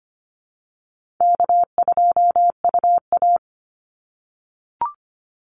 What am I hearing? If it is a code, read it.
Morse code, heard as a keyed tone, about 25 wpm, 700 Hz. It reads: K2UA
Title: Easy Call Signs - 25wpm